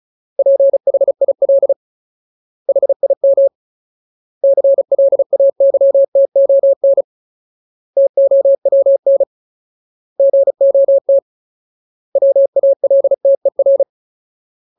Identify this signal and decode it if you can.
Morse code, heard as a keyed tone, about 34 wpm, 550 Hz. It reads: PHIL HIM CLAYTON TOWN GOT WALTER
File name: Ultimate Word List / 35WPM Ultimate